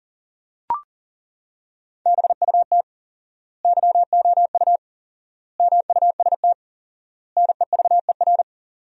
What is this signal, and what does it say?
Morse code, heard as a keyed tone, about 40 wpm, 700 Hz. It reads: BUT YOU MUST NEVER
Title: 4x Word-Spacing - Sentences from Top 300 Words - 40wpm